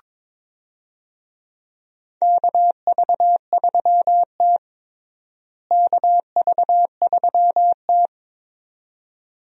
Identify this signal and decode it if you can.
Morse code, heard as a keyed tone, about 22 wpm, 700 Hz. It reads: KV3T KV3T
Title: USA Call Signs - 4 Characters - Repeated 2x - 22wpm